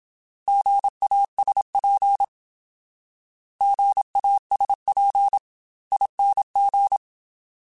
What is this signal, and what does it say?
Morse code, heard as a keyed tone, about 27 wpm, 800 Hz. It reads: GASP GASPING
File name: Alphabetical / Alphabetical mp3s / Gs @25